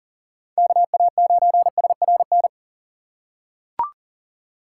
Morse code, heard as a keyed tone, about 38 wpm, 700 Hz. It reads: KA9SRN
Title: USA Call Signs - 6 Characters - 2x Character Spacing - 40wpm